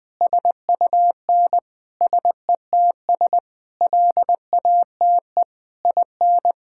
Morse code, heard as a keyed tone, about 20 wpm, 700 Hz. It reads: SUN SETS LATE IN
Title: Sentences with 8-Word Phrases - 20wpm